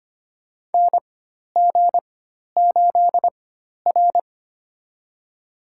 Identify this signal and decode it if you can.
Morse code, heard as a keyed tone, about 25 wpm, spaced as for 12 wpm, 700 Hz. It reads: NG8R
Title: USA Call Signs - 4 Characters - 4x Character Spacing - 25wpm